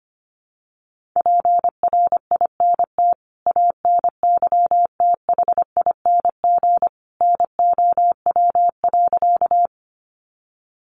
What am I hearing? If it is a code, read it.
Morse code, heard as a keyed tone, about 25 wpm, 700 Hz. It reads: PRINT ANYTHING NOW.
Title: Sentences from Top 1400 Words 25wpm